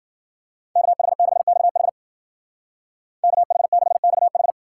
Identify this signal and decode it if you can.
Morse code, heard as a keyed tone, about 49 wpm, 700 Hz. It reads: KH6XH KH6XH
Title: Speed Racing - USA 2x2 Call Signs 40wpm